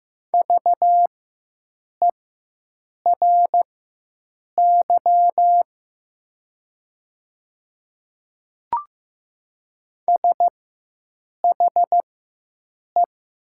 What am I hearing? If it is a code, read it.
Morse code, heard as a keyed tone, about 15 wpm, spaced as for 7 wpm, 700 Hz. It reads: VERY SHE
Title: 4x Character Spacing - Top 200 Words - 15wpm